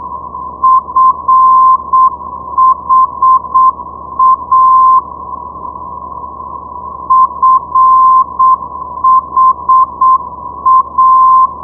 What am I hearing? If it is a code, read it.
Morse code, heard as a keyed tone, about 7 wpm, 1040 Hz. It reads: FHA FHA
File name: ndb friedrichshafen fha 473.0